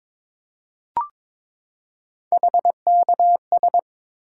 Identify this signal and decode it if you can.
Morse code, heard as a keyed tone, about 22 wpm, 700 Hz. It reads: HKS
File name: practice / 22wpm / Three Letter-Number Combinations - 22wpm